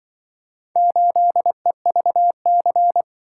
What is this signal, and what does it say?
Morse code, heard as a keyed tone, about 25 wpm, 700 Hz. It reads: 8EVC
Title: Speed-Racing - Four Letter-Number Combinations - 20wpm